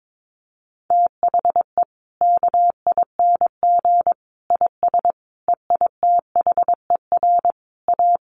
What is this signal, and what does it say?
Morse code, heard as a keyed tone, about 22 wpm, 700 Hz. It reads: THE KING IS EITHER A
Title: Sentences from Top 600 Words 22wpm